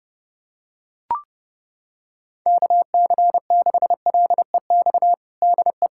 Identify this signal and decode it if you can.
Morse code, heard as a keyed tone, about 30 wpm, 700 Hz. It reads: KC6LEX DE
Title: USA Callsign DE Callsign 30wpm